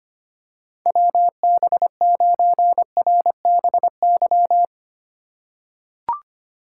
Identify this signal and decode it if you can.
Morse code, heard as a keyed tone, about 25 wpm, 700 Hz. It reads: WB9RBY
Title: USA Call Signs - 6 Characters - 2x Character Spacing - 25wpm